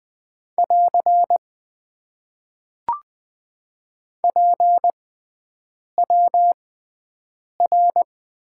Morse code, heard as a keyed tone, about 20 wpm, spaced as for 7 wpm, 700 Hz. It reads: <AR> PWR
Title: Top 90 QSO Elements - 6x Character Spacing - 20wpm